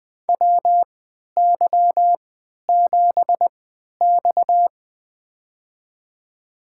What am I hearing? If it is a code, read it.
Morse code, heard as a keyed tone, about 20 wpm, spaced as for 11 wpm, 700 Hz. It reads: WY7X
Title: USA Call Signs - 4 Characters - 3x Character Spacing - 20wpm